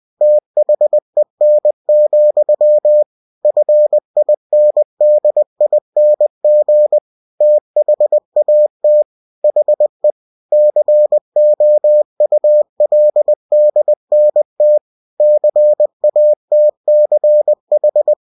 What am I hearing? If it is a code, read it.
Morse code, heard as a keyed tone, about 20 wpm, 600 Hz. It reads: THEN, FINDING THAT HE COULDNT CATCH
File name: cw-books / winnie-the-pooh / winnie-20wpm-page-0027